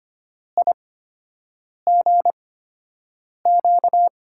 Morse code, heard as a keyed tone, about 25 wpm, spaced as for 7 wpm, 700 Hz. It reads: IGQ
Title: USA Call Signs - 8x Character Spacing - 25wpm